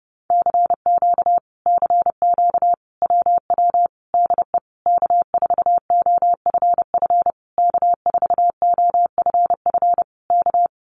Morse code, heard as a keyed tone, about 30 wpm, 700 Hz. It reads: CQ CQ WW DE K4OFF K4OFF K
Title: Common QSO Phrases Encore 30wpm